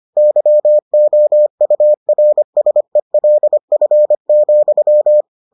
Morse code, heard as a keyed tone, about 25 wpm, 600 Hz. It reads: YOURSELF,